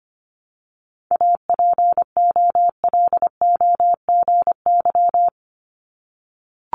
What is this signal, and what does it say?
Morse code, heard as a keyed tone, about 25 wpm, 700 Hz. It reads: APOLOGY
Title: Multi-letter Phonemes Repeated 3x 25wpm